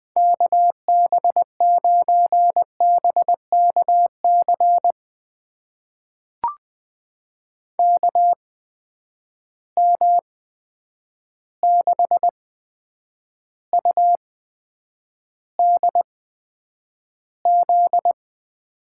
Morse code, heard as a keyed tone, about 20 wpm, spaced as for 9 wpm, 700 Hz. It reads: KB9BKC KM6UDZ
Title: USA Call Signs - 6 Characters - 8x Character Spacing - 20wpm